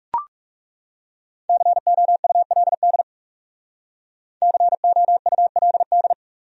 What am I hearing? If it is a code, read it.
Morse code, heard as a keyed tone, about 39 wpm, 700 Hz. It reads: COULD COULD